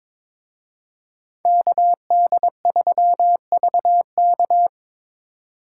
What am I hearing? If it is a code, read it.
Morse code, heard as a keyed tone, about 26 wpm, 700 Hz. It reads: KD3VK
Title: Speed Racing - USA 2x2 Call Signs 22wpm